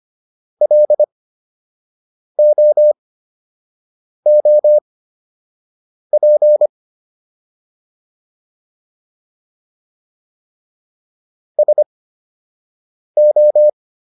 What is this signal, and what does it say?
Morse code, heard as a keyed tone, about 25 wpm, spaced as for 6 wpm, 600 Hz. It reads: LOOP SO